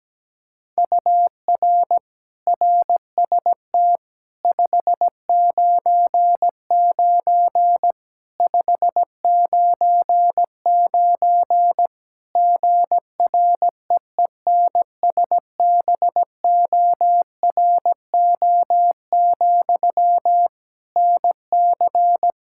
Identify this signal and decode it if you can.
Morse code, heard as a keyed tone, about 17 wpm, 700 Hz. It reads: UR RST 599 599 GREENSBORO, NC
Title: Practice QSO - Exchange 1 - Beginning - 17wpm